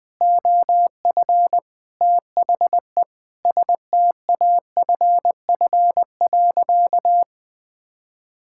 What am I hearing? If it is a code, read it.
Morse code, heard as a keyed tone, about 21 wpm, 700 Hz. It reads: OF THE STAFF.
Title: Sentences with 5-Word Phrases - Spoken Beginning - 20wpm